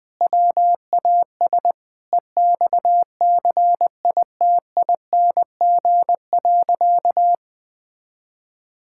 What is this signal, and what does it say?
Morse code, heard as a keyed tone, about 20 wpm, 700 Hz. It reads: WAS EXCITING.
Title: Sentences with 4-Word Phrases - 20wpm